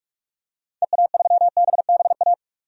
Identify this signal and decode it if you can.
Morse code, heard as a keyed tone, about 46 wpm, 700 Hz. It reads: EA3BBA
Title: DXCC-100 Call Signs - 6 Characters - 45wpm